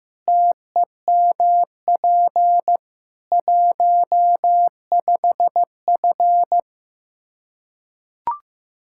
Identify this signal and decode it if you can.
Morse code, heard as a keyed tone, about 15 wpm, 700 Hz. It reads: TEMP 15F
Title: Practice QSO - Exchange 2 - Info - 15wpm